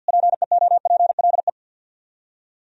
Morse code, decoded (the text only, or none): PEOPLE